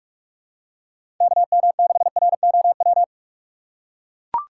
Morse code, heard as a keyed tone, about 44 wpm, 700 Hz. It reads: KM6ROW
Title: USA Call Signs - 3x Character Spacing - 45wpm